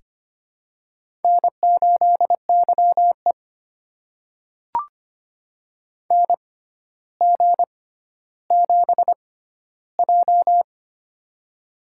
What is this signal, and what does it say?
Morse code, heard as a keyed tone, about 25 wpm, spaced as for 13 wpm, 700 Hz. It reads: N8YE NG7J
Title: USA Call Signs - 4 Characters - 6x Character Spacing - 25wpm